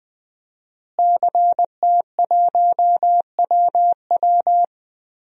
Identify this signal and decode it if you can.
Morse code, heard as a keyed tone, about 20 wpm, 700 Hz. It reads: CT1WW
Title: DXCC-100 Call Signs - 5 Characters - Repeated 3x - 20wpm